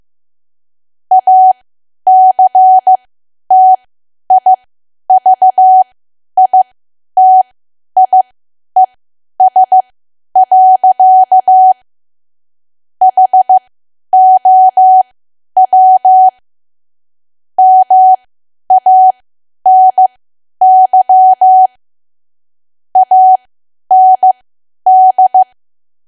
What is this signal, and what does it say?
Morse code, heard as a keyed tone, about 15 wpm, spaced as for 10 wpm, 750 Hz. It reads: ACTIVITIES. HOW MANY AND
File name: cwa / W1AW-Level2 / 10WPM